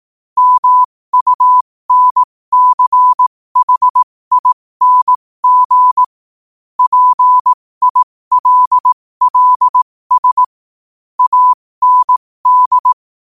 Morse code, heard as a keyed tone, about 18 wpm, spaced as for 15 wpm, 1000 Hz. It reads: MUNCHING PILLS AND
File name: qotdfiles / 15wpm